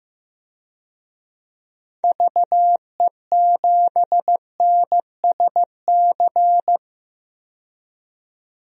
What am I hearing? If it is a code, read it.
Morse code, heard as a keyed tone, about 15 wpm, 700 Hz. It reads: VE7NSC
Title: DXCC-100 Call Signs - 6 Characters - 15wpm